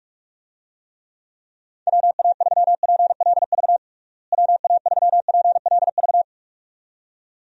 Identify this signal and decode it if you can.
Morse code, heard as a keyed tone, about 46 wpm, 700 Hz. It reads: WA3PLV WA3PLV
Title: USA Call Signs - 6 Characters - Repeated 2x - 45wpm